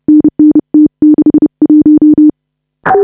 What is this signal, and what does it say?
Morse code, heard as a keyed tone, about 30 wpm, 300 Hz. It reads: NN T61
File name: AUDIO2019 / HA5BA CW